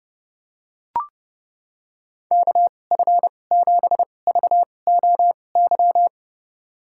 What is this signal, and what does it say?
Morse code, heard as a keyed tone, about 30 wpm, spaced as for 22 wpm, 700 Hz. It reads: KF7VOY
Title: USA Call Signs - 2x Character Spacing - 30wpm